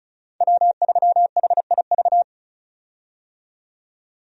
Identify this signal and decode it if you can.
Morse code, heard as a keyed tone, about 36 wpm, 700 Hz. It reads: W3HIV
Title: USA Call Signs - 5 Characters - 35wpm